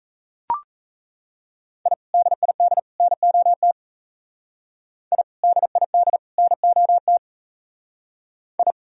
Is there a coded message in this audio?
Yes: Morse code from 1.9 to 8.7 s, heard as a keyed tone, about 35 wpm, 700 Hz. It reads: I DID NOT I DID NOT I